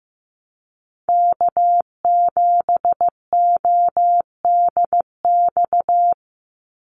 Morse code, heard as a keyed tone, about 15 wpm, 700 Hz. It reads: K7ODX
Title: NA SOTA Activator Callsigns 15wpm